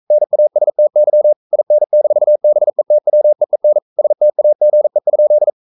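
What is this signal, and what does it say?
Morse code, heard as a keyed tone, about 42 wpm, 600 Hz. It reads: NASTY IN-BETWEEN STAGE?
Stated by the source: Morse code sending of an English novel at 42 wpm.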